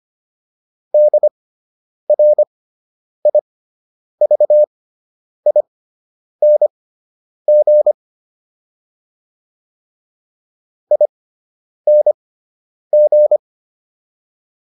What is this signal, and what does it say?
Morse code, heard as a keyed tone, about 25 wpm, spaced as for 9 wpm, 600 Hz. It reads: DRIVING ING